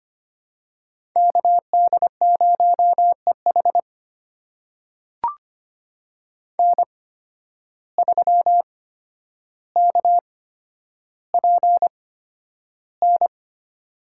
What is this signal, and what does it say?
Morse code, heard as a keyed tone, about 25 wpm, spaced as for 11 wpm, 700 Hz. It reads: KD0EH N3KPN
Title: USA Call Signs - 5 Characters - 8x Character Spacing - 25wpm